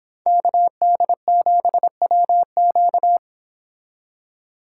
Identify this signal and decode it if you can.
Morse code, heard as a keyed tone, about 26 wpm, 700 Hz. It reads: KD7WQ
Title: Speed Racing - USA 2x2 Call Signs 22wpm